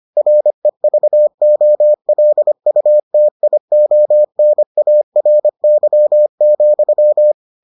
Morse code, heard as a keyed tone, about 25 wpm, 600 Hz. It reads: REVOLUTIONARY,